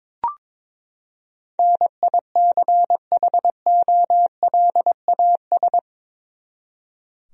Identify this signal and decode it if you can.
Morse code, heard as a keyed tone, about 22 wpm, 700 Hz. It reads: NICHOLAS
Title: USA Common Names - 22wpm